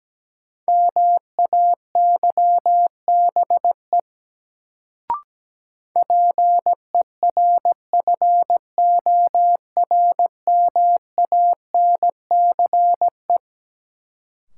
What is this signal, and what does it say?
Morse code, heard as a keyed tone, about 17 wpm, 700 Hz. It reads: MAYBE PERFORMANCE
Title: Sentences from Top 1000 Words - New Word Drill - 17wpm